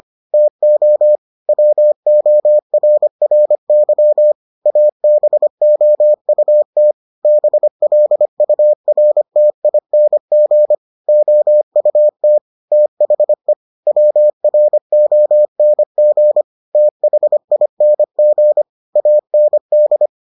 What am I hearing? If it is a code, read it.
Morse code, heard as a keyed tone, about 25 wpm, 600 Hz. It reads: TO WORRY ABOUT BLURTING OUT THE WRONG THING AND